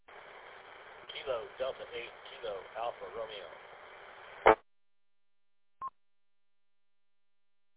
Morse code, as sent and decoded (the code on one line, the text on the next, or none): none